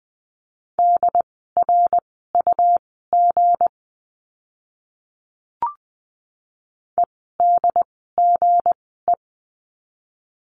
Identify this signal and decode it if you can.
Morse code, heard as a keyed tone, about 20 wpm, spaced as for 15 wpm, 700 Hz. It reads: DRUG EDGE
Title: 160 Lesson 26f - Letter g - Words 20wpm